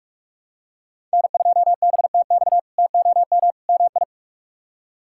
Morse code, heard as a keyed tone, about 46 wpm, 700 Hz. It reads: N2BTX TOM MI
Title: K1USN SST QSO Practice - 45wpm